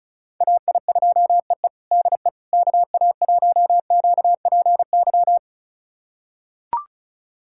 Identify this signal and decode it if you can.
Morse code, heard as a keyed tone, about 36 wpm, 700 Hz. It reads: AI2EE DE KA1QPY
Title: USA Callsign DE Callsign 35wpm